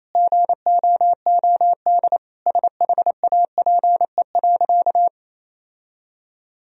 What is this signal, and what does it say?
Morse code, heard as a keyed tone, about 27 wpm, 690 Hz. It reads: GOOD SHAPE.
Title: Sentences from Top 1000 Words - Review Part 3 - 28wpm